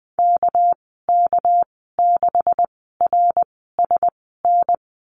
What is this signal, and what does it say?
Morse code, heard as a keyed tone, about 20 wpm, spaced as for 15 wpm, 700 Hz. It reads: KK6RSN
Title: 200 Lesson 32g - Letter k - Call Signs 20wpm